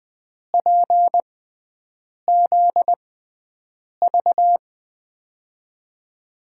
Morse code, heard as a keyed tone, about 20 wpm, spaced as for 7 wpm, 700 Hz. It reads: PZV
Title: USA Call Signs - 6 Characters - 6x Character Spacing - 20wpm